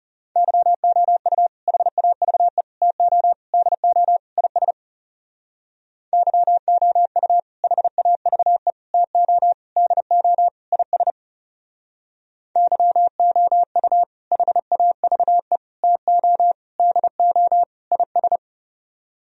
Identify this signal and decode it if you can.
Morse code, heard as a keyed tone, about 35 wpm, 700 Hz. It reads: YOU HAVE TO DO IS YOU HAVE TO DO IS YOU HAVE TO DO IS
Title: Speed-Racing - Sets of 5 Words 30wpm